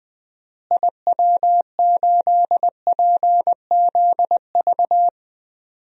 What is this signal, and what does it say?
Morse code, heard as a keyed tone, about 20 wpm, 700 Hz. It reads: IW8PZV
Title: DXCC-100 Call Signs - 6 Characters - Repeated 3x - 20wpm